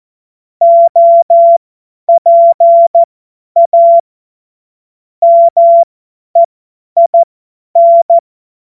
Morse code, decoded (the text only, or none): OPA MEIN